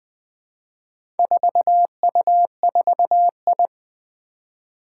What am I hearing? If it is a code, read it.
Morse code, heard as a keyed tone, about 21 wpm, 700 Hz. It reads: 4U4I
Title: Four Challenging Letter-Number Combinations 20wpm